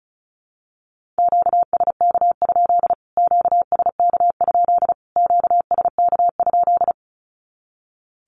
Morse code, heard as a keyed tone, about 34 wpm, 700 Hz. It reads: QSK? QSK? QSK?
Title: Q-Codes Repeated 3x 35wpm